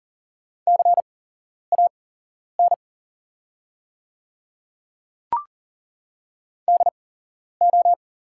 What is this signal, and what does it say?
Morse code, heard as a keyed tone, about 40 wpm, spaced as for 11 wpm, 700 Hz. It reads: CAN DO
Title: 8x Character Spacing - Top 100 Words - 40wpm